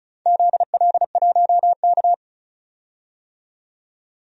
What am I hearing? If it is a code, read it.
Morse code, heard as a keyed tone, about 36 wpm, 700 Hz. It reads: ZL1K
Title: DXCC-100 Call Signs - 4 Characters - 35wpm